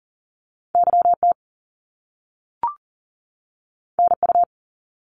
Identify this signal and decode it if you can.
Morse code, heard as a keyed tone, about 41 wpm, 700 Hz. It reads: YT NU
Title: Canadian Provinces and Territories 40wpm